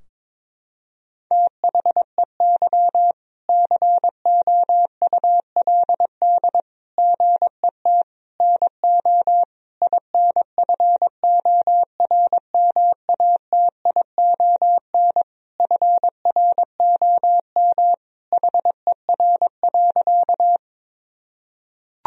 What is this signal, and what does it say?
Morse code, heard as a keyed tone, about 22 wpm, 700 Hz. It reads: THEY COULD GET NO INFORMATION FROM HER.
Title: Sentences from Top 1000 Words - Review Part 3 - 22wpm